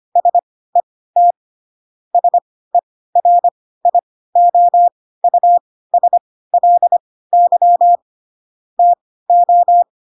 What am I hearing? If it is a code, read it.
Morse code, heard as a keyed tone, about 25 wpm, spaced as for 16 wpm, 700 Hz. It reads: SET SERIOUSLY TO
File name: Pinocchio / Pinocchio ch3 10 25 ik1hge